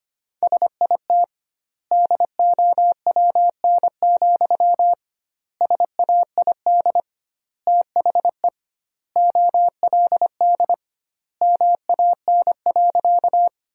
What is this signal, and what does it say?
Morse code, heard as a keyed tone, about 25 wpm, 700 Hz. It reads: SIT DOWN, SAID THE OLD MAN.
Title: 2x Word-Spacing - Sentences from Top 500 Words - 25wpm